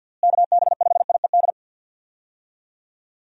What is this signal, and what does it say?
Morse code, heard as a keyed tone, about 45 wpm, 700 Hz. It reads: KB5IED
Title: USA Call Signs - 6 Characters - Repeated 2x - 50wpm